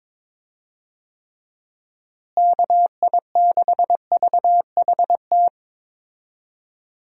Morse code, heard as a keyed tone, about 22 wpm, 700 Hz. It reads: KI6VHT
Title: USA Call Signs - 6 Characters - Repeated 3x - 22wpm